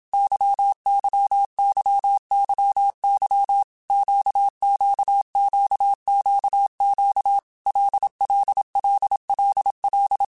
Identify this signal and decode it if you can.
Morse code, heard as a keyed tone, about 27 wpm, 800 Hz. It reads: YYYYY QQQQQ LLLLL
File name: Basic Letter Groups By7Groups / LFCYQ / LFCYQ@25